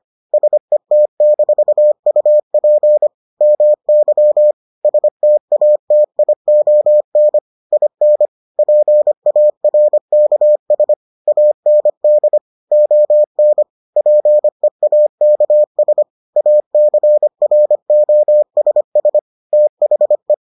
Morse code, heard as a keyed tone, about 25 wpm, 600 Hz. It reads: SET-UP MY STATION IN PARKS AND ON PEAKS ACROSS THE